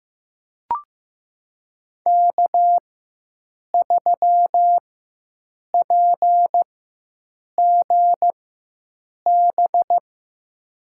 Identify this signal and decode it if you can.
Morse code, heard as a keyed tone, about 15 wpm, spaced as for 7 wpm, 700 Hz. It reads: K3PGB
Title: USA Call Signs - 5 Characters - 4x Character Spacing - 15wpm